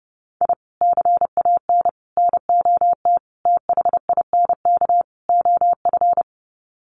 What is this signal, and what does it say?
Morse code, heard as a keyed tone, about 30 wpm, 700 Hz. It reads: I CAN NOT THINK OF
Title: Sets of 5 Words 30wpm